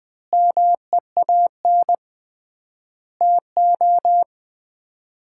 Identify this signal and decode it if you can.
Morse code, heard as a keyed tone, about 21 wpm, 700 Hz. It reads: MEAN TO
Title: 3x Word-Spacing - Sets of 5 Words - 20wpm